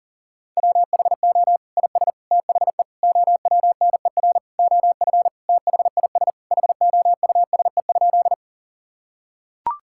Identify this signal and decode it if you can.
Morse code, heard as a keyed tone, about 37 wpm, 700 Hz. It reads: WHO IS THE OWNER OF THIS HOUSE?
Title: Sentences from Top 1600 Words 40wpm